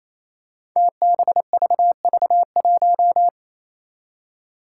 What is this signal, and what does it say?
Morse code, heard as a keyed tone, about 28 wpm, 700 Hz. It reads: TBVV1
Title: Five Challenging Letter-Number Combinations 28wpm